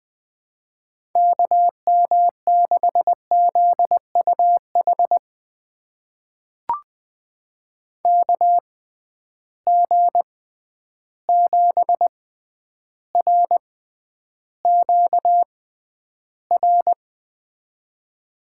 Morse code, heard as a keyed tone, about 21 wpm, spaced as for 10 wpm, 700 Hz. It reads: KM6ZUH KG7RQR
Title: USA Call Signs - 6 Characters - 6x Character Spacing - 20wpm